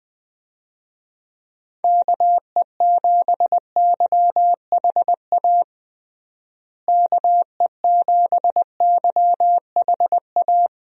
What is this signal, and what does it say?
Morse code, heard as a keyed tone, about 20 wpm, 700 Hz. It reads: KE7YHA KE7YHA
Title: USA Call Signs - 6 Characters - Repeated 3x - 20wpm